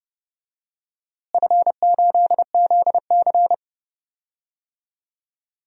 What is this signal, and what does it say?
Morse code, heard as a keyed tone, about 30 wpm, 700 Hz. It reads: F8ZC
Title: DXCC-100 Call Signs - 4 Characters - Repeated 3x - 30wpm